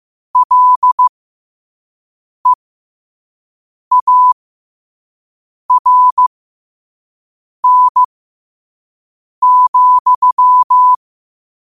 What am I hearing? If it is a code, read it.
Morse code, heard as a keyed tone, about 15 wpm, spaced as for 5 wpm, 1000 Hz. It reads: LEARN,
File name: qotdfiles / 5wpm